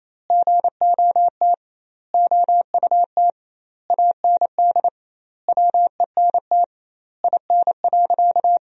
Morse code, heard as a keyed tone, about 28 wpm, 700 Hz. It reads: GOT OUT AND WENT IN.
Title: 2x Word-Spacing - Sentences from Top 300 Words - 28wpm